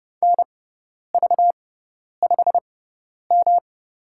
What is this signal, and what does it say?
Morse code, heard as a keyed tone, about 31 wpm, spaced as for 10 wpm, 700 Hz. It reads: NV5M
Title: USA Call Signs - 4 Characters - 6x Character Spacing - 30wpm